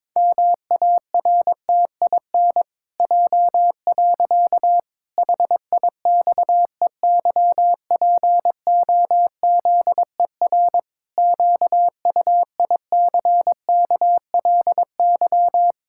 Morse code, heard as a keyed tone, about 22 wpm, 700 Hz. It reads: MARTIN J. HIXEYPOZER QUICKLY
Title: Pangram Challenge v2 - 22wpm